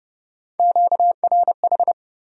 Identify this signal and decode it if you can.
Morse code, heard as a keyed tone, about 29 wpm, 700 Hz. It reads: QRH